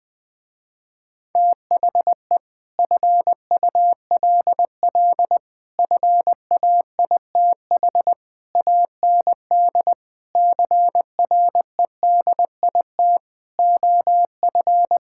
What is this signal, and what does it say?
Morse code, heard as a keyed tone, about 21 wpm, 700 Hz. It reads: THE FULL FAITH AND CREDIT OF
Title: Sentences with 7-Word Phrases - Spoken Beginning - 20wpm